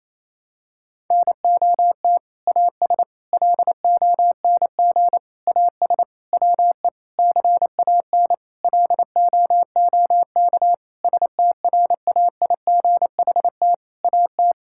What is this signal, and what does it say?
Morse code, heard as a keyed tone, about 28 wpm, 700 Hz. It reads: NOT AS LONG AS WE CAN LOOK STRAIGHT AT
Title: Sentences from Top 1000 Words - Review Part 6 - 28wpm